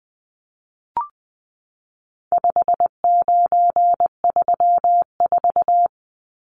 Morse code, heard as a keyed tone, about 20 wpm, 700 Hz. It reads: 5934
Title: Numbers - 4-Digit 20wpm